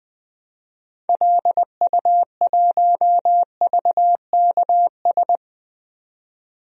Morse code, heard as a keyed tone, about 20 wpm, 700 Hz. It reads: LU1VKS
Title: DXCC-100 Call Signs - 6 Characters - 20wpm